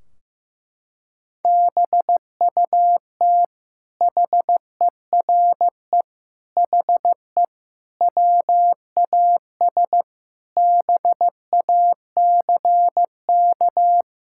Morse code, heard as a keyed tone, about 15 wpm, 700 Hz. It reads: BUT HERE HE WAS BACK
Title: Sentences from Top 1600 Words 15wpm